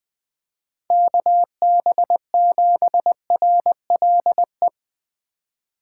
Morse code, heard as a keyed tone, about 20 wpm, 700 Hz. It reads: KB7RLE
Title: USA Call Signs - 6 Characters - 2x Character Spacing - 20wpm